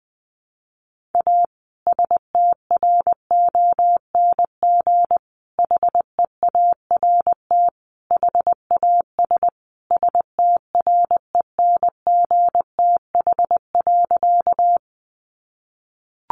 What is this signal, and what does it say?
Morse code, heard as a keyed tone, about 20 wpm, 700 Hz. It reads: A STRONG HEART HAS STRENGTH.
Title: Sentences from Top 1300 Words 20wpm